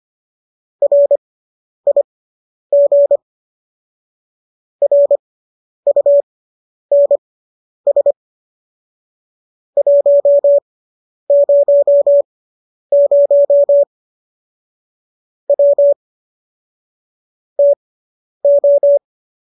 Morse code, heard as a keyed tone, about 25 wpm, spaced as for 10 wpm, 570 Hz. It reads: RIG RUNS 100 W TO